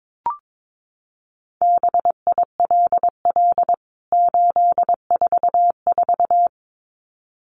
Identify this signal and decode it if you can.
Morse code, heard as a keyed tone, about 22 wpm, 700 Hz. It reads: BILL 844
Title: CWT Exchange 22wpm